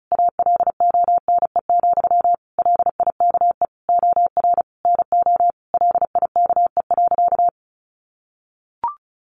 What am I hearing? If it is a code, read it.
Morse code, heard as a keyed tone, about 34 wpm, 700 Hz. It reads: ALONE, LIKE OR NO LIKE.
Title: Sentences from Top 1000 Words 35wpm